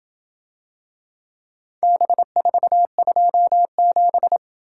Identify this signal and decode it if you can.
Morse code, heard as a keyed tone, about 27 wpm, 700 Hz. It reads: B427